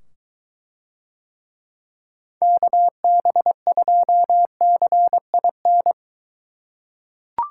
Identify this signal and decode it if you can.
Morse code, heard as a keyed tone, about 21 wpm, 700 Hz. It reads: KB2CIN